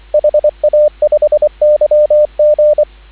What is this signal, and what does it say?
Morse code, heard as a keyed tone, about 24 wpm, 590 Hz, with radio noise. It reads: HA5YG